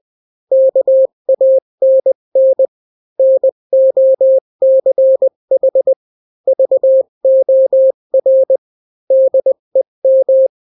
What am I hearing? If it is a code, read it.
Morse code, heard as a keyed tone, about 20 wpm, 530 Hz. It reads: KANN NOCH VOR DEM
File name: daily / 18wpm